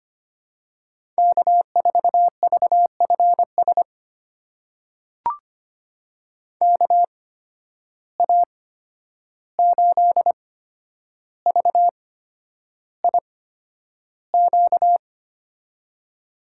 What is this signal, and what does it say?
Morse code, heard as a keyed tone, about 25 wpm, spaced as for 10 wpm, 700 Hz. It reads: K4VFS KA8VIQ
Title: USA Call Signs - 8x Character Spacing - 25wpm